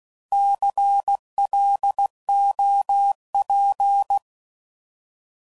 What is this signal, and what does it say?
Morse code, heard as a keyed tone, about 16 wpm, 800 Hz. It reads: CLOP
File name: Prefixes / By First Letter / Cs